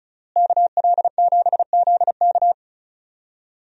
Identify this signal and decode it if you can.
Morse code, heard as a keyed tone, about 36 wpm, 700 Hz. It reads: KL7ZK
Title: USA Call Signs - 5 Characters - Repeated 3x - 35wpm